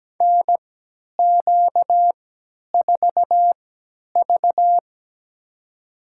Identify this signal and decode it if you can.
Morse code, heard as a keyed tone, about 17 wpm, spaced as for 10 wpm, 700 Hz. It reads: NQ4V
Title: USA Call Signs - 3x Character Spacing - 17wpm